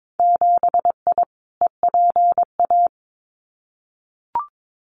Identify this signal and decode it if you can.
Morse code, heard as a keyed tone, about 22 wpm, 700 Hz. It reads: 7I EPA
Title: Winter Field Day Exchanges Encore 22wpm